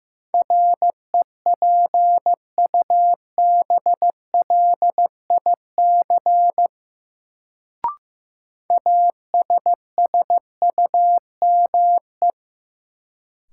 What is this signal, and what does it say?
Morse code, heard as a keyed tone, about 15 wpm, 700 Hz. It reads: REPUBLIC ASSUME
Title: Sentences from Top 1500 Words - New Word Drill Alt - 15wpm